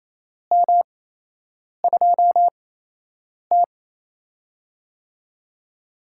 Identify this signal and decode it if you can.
Morse code, heard as a keyed tone, about 27 wpm, spaced as for 8 wpm, 700 Hz. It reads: M2T
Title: USA Call Signs - 4 Characters - 8x Character Spacing - 28wpm